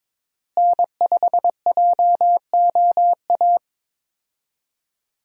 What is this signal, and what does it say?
Morse code, heard as a keyed tone, about 21 wpm, 700 Hz. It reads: N5JOA
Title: USA Call Signs - 5 Characters - Repeated 3x - 22wpm